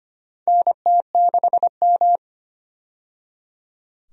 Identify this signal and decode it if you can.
Morse code, heard as a keyed tone, about 25 wpm, 700 Hz. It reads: NT6M
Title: USA Call Signs - 4 Characters - 25wpm